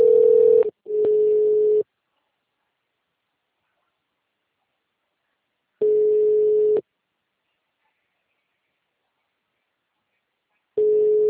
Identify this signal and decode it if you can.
Ringback tone (Europe, 425 Hz).
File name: out- -2005- - - .13100